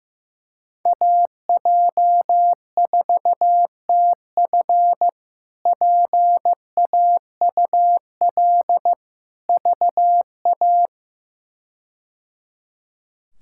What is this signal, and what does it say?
Morse code, heard as a keyed tone, about 15 wpm, 700 Hz. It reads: AJ4TF PAUL VA
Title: K1USN SST QSO Practice - 15wpm